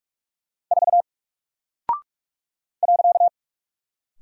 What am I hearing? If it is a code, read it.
Morse code, heard as a keyed tone, about 45 wpm, 700 Hz. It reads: 4 .
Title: Single Character (Letter:Number:Punctuation:Prosign) Alt 45wpm